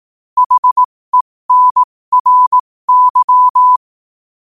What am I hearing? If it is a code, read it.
Morse code, heard as a keyed tone, about 18 wpm, spaced as for 15 wpm, 1000 Hz. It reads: HENRY
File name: qotdfiles / 15wpm